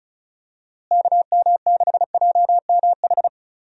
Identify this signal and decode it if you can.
Morse code, heard as a keyed tone, about 34 wpm, 700 Hz. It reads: KM6JMH
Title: USA Call Signs - 6 Characters - Repeated 3x - 35wpm